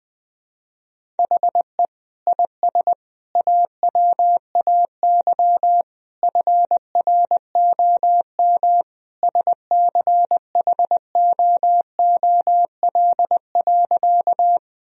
Morse code, heard as a keyed tone, about 20 wpm, 700 Hz. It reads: HE IS AWAY FROM SCHOOL.
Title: Sentences from Top 1000 Words - Review Part 1 - 20wpm